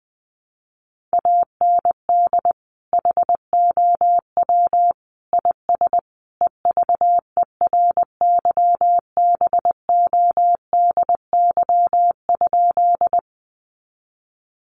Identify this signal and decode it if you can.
Morse code, heard as a keyed tone, about 20 wpm, 700 Hz. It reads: AND HOW IS EVERYBODY?
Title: Sentences from Top 1400 Words 20wpm